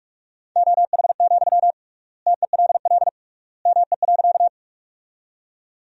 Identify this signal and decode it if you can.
Morse code, heard as a keyed tone, about 41 wpm, 700 Hz. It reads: OH, TELL ME.
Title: 3x Word-Spacing - Sentences from Top 500 Words - 45wpm